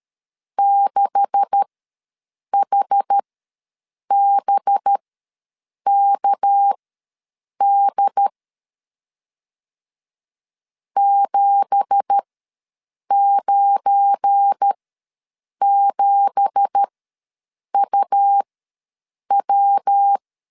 Morse code, heard as a keyed tone, about 13 wpm, spaced as for 7 wpm, 800 Hz. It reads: 6HBKD 797UW
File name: LEZIONI / 9 2867 ws 12wpm 3431